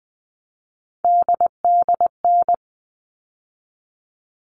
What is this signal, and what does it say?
Morse code, heard as a keyed tone, about 20 wpm, 700 Hz. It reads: DDN
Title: All Cut Numbers - 3-Digit 20wpm